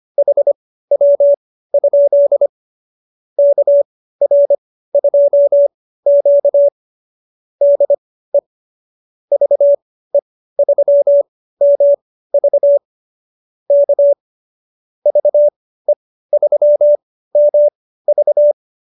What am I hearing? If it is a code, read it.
Morse code, heard as a keyed tone, about 25 wpm, spaced as for 15 wpm, 590 Hz. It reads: HW? KR2Q DE VE3MV K VE3MV